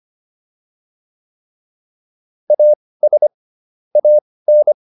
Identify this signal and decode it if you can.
Morse code, heard as a keyed tone, about 25 wpm, spaced as for 18 wpm, 600 Hz. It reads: AS AN